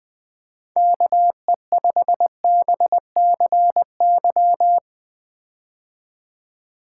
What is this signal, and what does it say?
Morse code, heard as a keyed tone, about 20 wpm, 700 Hz. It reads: KE5BCY
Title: K1USN SST QSO Practice - 20wpm